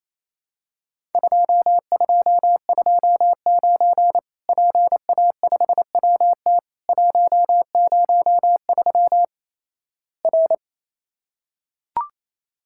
Morse code, heard as a keyed tone, about 28 wpm, 700 Hz. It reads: 2229 PA5WT 103 R
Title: Worked All Europe DX Contest - QTC Exchange - 28wpm